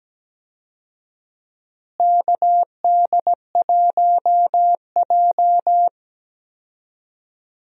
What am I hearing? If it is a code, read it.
Morse code, heard as a keyed tone, about 17 wpm, 700 Hz. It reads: KD1J
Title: USA Call Signs - 4 Characters - Repeated 2x - 17wpm